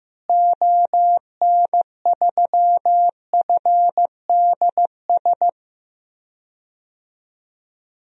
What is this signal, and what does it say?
Morse code, heard as a keyed tone, about 16 wpm, 700 Hz. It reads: ON3FDS
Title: DXCC-100 Call Signs - 6 Characters - Repeated 2x - 15wpm